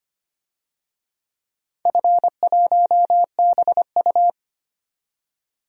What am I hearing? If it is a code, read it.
Morse code, heard as a keyed tone, about 25 wpm, 700 Hz. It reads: F1BU
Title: DXCC-100 Call Signs - 4 Characters - 25wpm